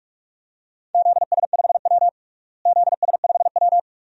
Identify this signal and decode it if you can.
Morse code, heard as a keyed tone, about 41 wpm, 700 Hz. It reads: ZS5W ZS5W
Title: DXCC-100 Call Signs - 4 Characters - Repeated 2x - 45wpm